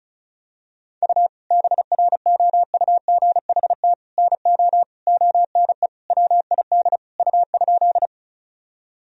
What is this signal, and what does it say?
Morse code, heard as a keyed tone, about 34 wpm, 700 Hz. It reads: U BROUGHT NO ONE WID U?
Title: Sentences from Top 500 Words and Abbreviations 35wpm